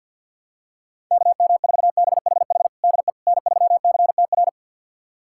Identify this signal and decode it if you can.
Morse code, heard as a keyed tone, about 50 wpm, 700 Hz. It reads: KM4BHH DE N3CTR
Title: USA Callsign DE Callsign 50wpm